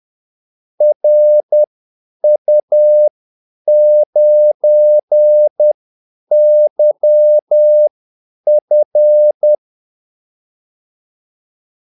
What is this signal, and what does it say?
Morse code, heard as a keyed tone, about 10 wpm, spaced as for 8 wpm, 600 Hz. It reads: RU9YF
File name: Callsigns-10-8wpm